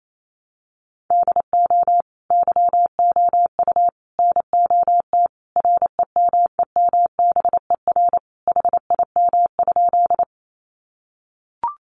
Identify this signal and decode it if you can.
Morse code, heard as a keyed tone, about 28 wpm, 700 Hz. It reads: DO YOU NOT REMEMBER HIM?
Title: Sentences from Top 600 Words 28wpm